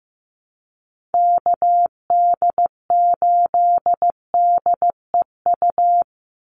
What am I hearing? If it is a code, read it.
Morse code, heard as a keyed tone, about 15 wpm, 700 Hz. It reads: KD8DEU
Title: NA SOTA Chaser Callsigns 15wpm